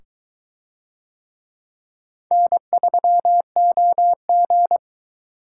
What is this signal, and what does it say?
Morse code, heard as a keyed tone, about 22 wpm, 700 Hz. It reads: N3OG